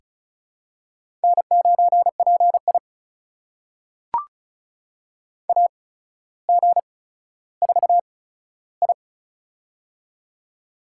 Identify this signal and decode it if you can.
Morse code, heard as a keyed tone, about 36 wpm, spaced as for 15 wpm, 700 Hz. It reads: N9PI AG4I
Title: USA Call Signs - 4 Characters - 8x Character Spacing - 35wpm